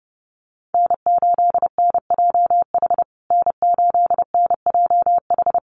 Morse code, heard as a keyed tone, about 30 wpm, 700 Hz. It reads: N8NJH N8NJH
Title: USA Call Signs Repeated 3x 30wpm